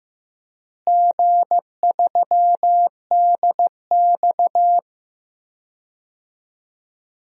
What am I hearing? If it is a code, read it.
Morse code, heard as a keyed tone, about 16 wpm, 700 Hz. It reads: G3DX
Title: DXCC-100 Call Signs - 4 Characters - Repeated 2x - 15wpm